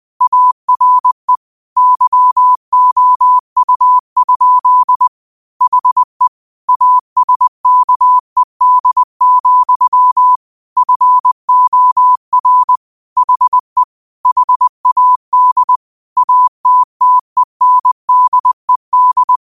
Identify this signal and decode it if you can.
Morse code, heard as a keyed tone, about 20 wpm, 1000 Hz. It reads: ARE YOU? HE ASKED, FOR HE HAD ATTENDED
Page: Quotes for Sat, 16 Aug 2025 in Morse Code at 20 words per minute.